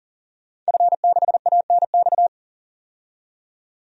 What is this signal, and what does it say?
Morse code, heard as a keyed tone, about 40 wpm, 700 Hz. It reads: F6ANX
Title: DXCC-100 Call Signs - 5 Characters - 40wpm